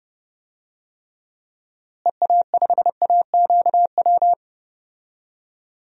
Morse code, heard as a keyed tone, about 30 wpm, 700 Hz. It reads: EA5AQW
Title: DXCC-100 Call Signs - 6 Characters - Repeated 3x - 30wpm